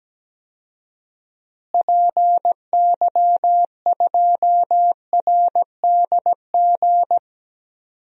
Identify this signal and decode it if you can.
Morse code, heard as a keyed tone, about 17 wpm, 700 Hz. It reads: PY2RDG
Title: DXCC-100 Call Signs - 6 Characters - Repeated 3x - 17wpm